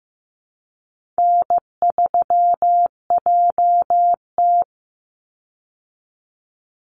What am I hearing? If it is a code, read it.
Morse code, heard as a keyed tone, about 14 wpm, 700 Hz. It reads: N3JT
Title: Easy Call Signs Repeated 3x - 15wpm